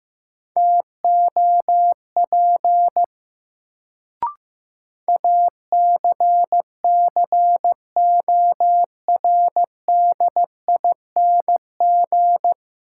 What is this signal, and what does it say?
Morse code, heard as a keyed tone, about 15 wpm, 700 Hz. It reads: TOP ACCORDING
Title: Sentences from Top 700 Words - New Word Drill - 15wpm